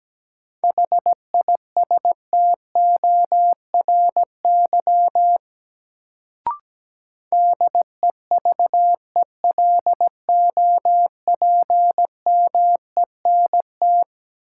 Morse code, heard as a keyed tone, about 17 wpm, 700 Hz. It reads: HISTORY DEVELOPMENT
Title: Sets of 3 Words - New Word Drill - 17wpm